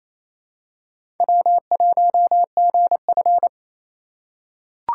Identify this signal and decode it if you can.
Morse code, heard as a keyed tone, about 27 wpm, 700 Hz. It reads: W1GF
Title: USA Call Signs - 4 Characters - 3x Character Spacing - 28wpm